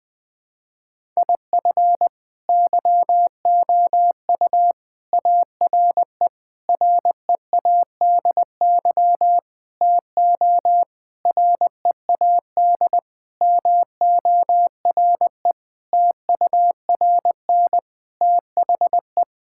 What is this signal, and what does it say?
Morse code, heard as a keyed tone, about 19 wpm, 700 Hz. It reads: IF YOU ARE READY TO READ MORE TURN THE
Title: Sentences with 6-Word Phrases - 20wpm